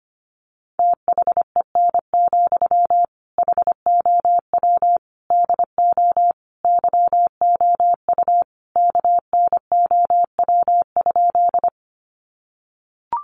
Morse code, heard as a keyed tone, about 25 wpm, 700 Hz. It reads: THEN, HOW DO YOU KNOW?
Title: Sentences from Top 100 Words 25wpm